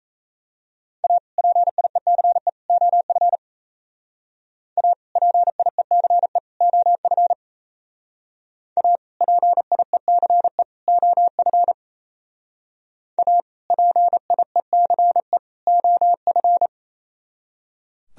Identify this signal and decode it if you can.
Morse code, heard as a keyed tone, about 35 wpm, 700 Hz. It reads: A PIECE OF A PIECE OF A PIECE OF A PIECE OF
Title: Speed-Racing - Sets of 3 Words 28wpm